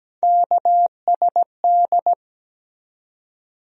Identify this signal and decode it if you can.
Morse code, heard as a keyed tone, about 17 wpm, 700 Hz. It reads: KSD
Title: USA Call Sign Postfixes - 3 Letters 17wpm